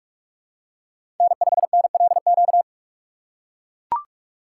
Morse code, heard as a keyed tone, about 41 wpm, 700 Hz. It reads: N5NLQ
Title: USA Call Signs - 5 Characters - 2x Character Spacing - 45wpm